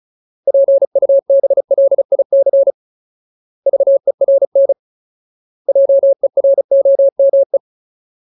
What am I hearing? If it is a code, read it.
Morse code, heard as a keyed tone, about 34 wpm, 550 Hz. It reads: PUBLIC VERN JEROME
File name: Ultimate Word List / 35WPM Ultimate